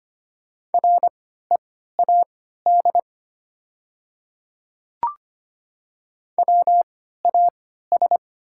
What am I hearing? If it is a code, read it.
Morse code, heard as a keyed tone, about 25 wpm, spaced as for 14 wpm, 690 Hz. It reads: READ WAS
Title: 3x Character Spacing - Top 500 Words - 25wpm